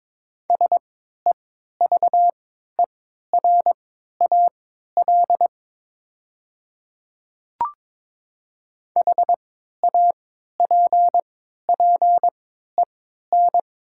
Morse code, heard as a keyed tone, about 22 wpm, spaced as for 12 wpm, 700 Hz. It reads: SEVERAL HAPPEN
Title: 3x Character Spacing - Top 500 Words - 22wpm